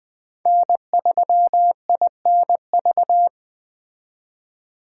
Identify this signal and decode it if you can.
Morse code, heard as a keyed tone, about 20 wpm, 700 Hz. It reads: N3INV
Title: Speed Racing - USA 1x3 Call Signs 17wpm